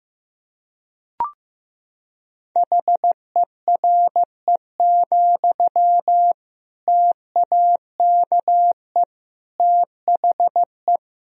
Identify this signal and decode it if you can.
Morse code, heard as a keyed tone, about 15 wpm, 700 Hz. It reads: HERE, TAKE THE
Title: Sentences from Top 1000 Words - Review Part 1 - 15wpm